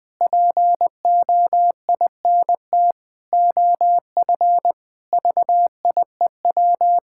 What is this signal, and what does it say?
Morse code, heard as a keyed tone, about 19 wpm, 700 Hz. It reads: POINT OF VIEW
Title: Sentences from Top 1000 Words - Review Part 6 - 20wpm